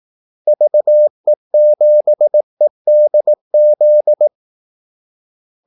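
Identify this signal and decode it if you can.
Morse code, heard as a keyed tone, about 18 wpm, 600 Hz. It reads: VE7EDZ
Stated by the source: Morse call-sign practice at 18 wpm.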